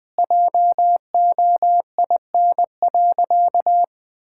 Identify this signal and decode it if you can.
Morse code, heard as a keyed tone, about 21 wpm, 700 Hz. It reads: JOIN.
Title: Sentences with 5-Word Phrases - 20wpm